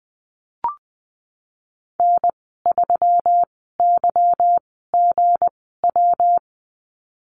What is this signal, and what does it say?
Morse code, heard as a keyed tone, about 20 wpm, spaced as for 15 wpm, 700 Hz. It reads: N3YGW
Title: 186 Lesson 30g - Letter b - Call Signs 20wpm